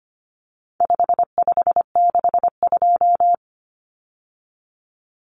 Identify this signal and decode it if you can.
Morse code, heard as a keyed tone, about 25 wpm, 700 Hz. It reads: 5562
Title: Common Cut Numbers - 4-Digit 25wpm